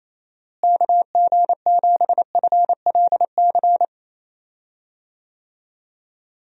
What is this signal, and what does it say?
Morse code, heard as a keyed tone, about 28 wpm, 700 Hz. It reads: KG7FLC
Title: USA Call Signs - 6 Characters - Repeated 2x - 28wpm